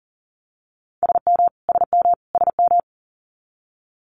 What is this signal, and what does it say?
Morse code, heard as a keyed tone, about 42 wpm, 700 Hz. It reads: SM SM SM
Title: Multi-letter Phonemes Repeated 3x 40wpm